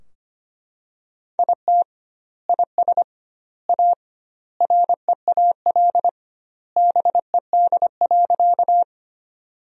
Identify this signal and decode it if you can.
Morse code, heard as a keyed tone, about 25 wpm, 700 Hz. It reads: IT IS A REAL BED.
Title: 2x Word-Spacing - Sentences from Top 500 Words - 25wpm